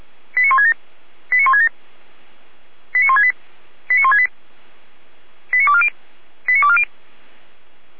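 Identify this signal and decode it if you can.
Five-tone selective call (ZVEI), heard as quick runs of five tones, tones of about 70 ms. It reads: 89167 89167 89168 89168 89240 89240
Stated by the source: ZVEI five-tone selective call 89115 received via SDR.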